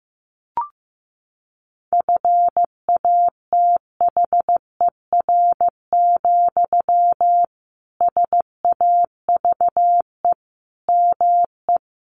Morse code, heard as a keyed tone, about 15 wpm, 700 Hz. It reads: FATHER, SAVE ME
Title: Sentences from Top 1100 Words 15wpm